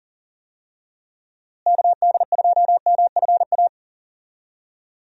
Morse code, heard as a keyed tone, about 40 wpm, 700 Hz. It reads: KD2MFA
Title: USA Call Signs - 6 Characters - Repeated 3x - 40wpm